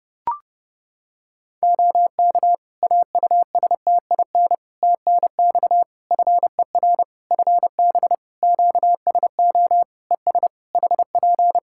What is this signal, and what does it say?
Morse code, heard as a keyed tone, about 31 wpm, 700 Hz. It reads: OK AUSTIN TNX FER FB QSO ES HP
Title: Practice QSO - Exchange 4 - Ending - 30wpm